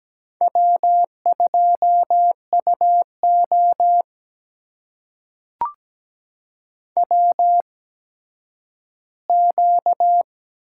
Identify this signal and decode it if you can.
Morse code, heard as a keyed tone, about 17 wpm, spaced as for 10 wpm, 700 Hz. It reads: W2UO WQ
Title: USA Call Signs - 4 Characters - 8x Character Spacing - 17wpm